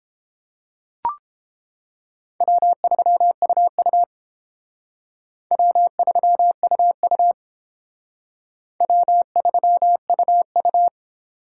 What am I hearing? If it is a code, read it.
Morse code, heard as a keyed tone, about 29 wpm, 700 Hz. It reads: W3UU W3UU W3UU